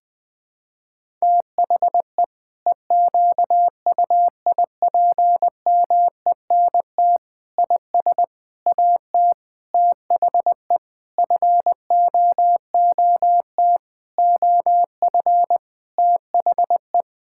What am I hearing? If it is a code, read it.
Morse code, heard as a keyed tone, about 20 wpm, 700 Hz. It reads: THE EQUIPMENT IS AT THE FOOT OF THE
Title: Sentences from Top 1000 Words - Review Part 5 - 20wpm